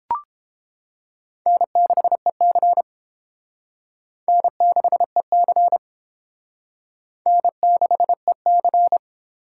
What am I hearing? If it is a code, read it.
Morse code, heard as a keyed tone, about 29 wpm, 700 Hz. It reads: N6EC N6EC N6EC